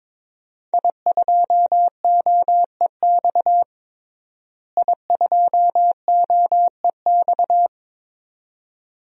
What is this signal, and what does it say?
Morse code, heard as a keyed tone, about 22 wpm, 700 Hz. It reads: I2OEX I2OEX
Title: DXCC-100 Call Signs - 5 Characters - Repeated 2x - 22wpm